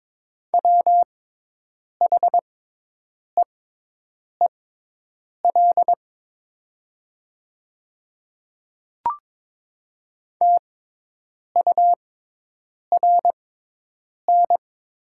Morse code, heard as a keyed tone, about 22 wpm, spaced as for 8 wpm, 700 Hz. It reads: WHEEL TURN
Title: 6x Character Spacing - Top 500 Words - 22wpm